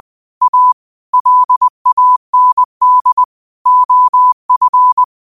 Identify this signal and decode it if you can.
Morse code, heard as a keyed tone, about 20 wpm, 1000 Hz. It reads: A LAND OF
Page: New quotes every day in morse code at 20 Words per minute.